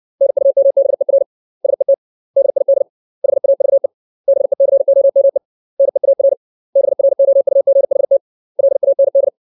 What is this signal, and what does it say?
Morse code, heard as a keyed tone, about 60 wpm, 550 Hz. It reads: NUMBER SET DID HAVE BECOME NEAR BROUGHT LAND
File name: 632 words / 632 Words 60WPM